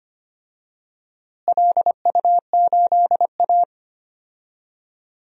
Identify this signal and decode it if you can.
Morse code, heard as a keyed tone, about 25 wpm, 700 Hz. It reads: LU8A
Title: DXCC-100 Call Signs - 4 Characters - Repeated 3x - 25wpm